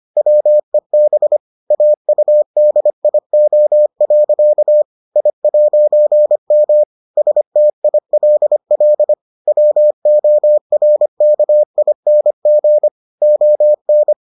Morse code, heard as a keyed tone, about 25 wpm, 600 Hz. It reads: WEB AUDIO. I'M STILL WORKING ON